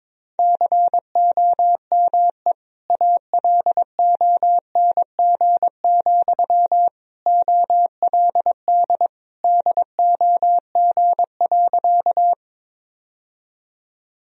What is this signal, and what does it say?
Morse code, heard as a keyed tone, about 22 wpm, 700 Hz. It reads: COME ALONG, OLD DOG.
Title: Sentences from Top 1000 Words - Review Part 2 - 22wpm